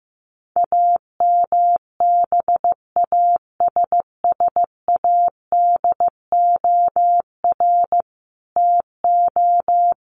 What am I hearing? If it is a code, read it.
Morse code, heard as a keyed tone, about 15 wpm, 700 Hz. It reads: AMBASSADOR TO
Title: Sets of 5 Words Encore2 15wpm